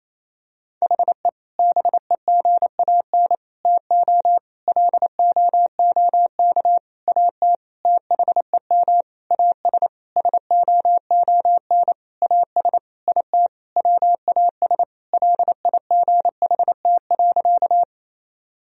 Morse code, heard as a keyed tone, about 28 wpm, 700 Hz. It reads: HE BEGAN TO LOOK AT THEM AS SOON AS IT WAS LIGHT.
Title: Sentences from Top 1000 Words - Review Part 1 - 28wpm